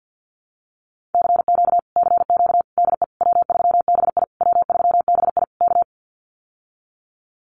Morse code, heard as a keyed tone, about 51 wpm, 700 Hz. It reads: CQ CQ DE W3BI W3BI K
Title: Common QSO Phrases Encore 50wpm